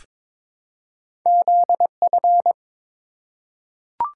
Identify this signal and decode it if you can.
Morse code, heard as a keyed tone, about 23 wpm, 700 Hz. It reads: ZF
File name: practice / 22wpm / Two Letter-Number Combinations - 22wpm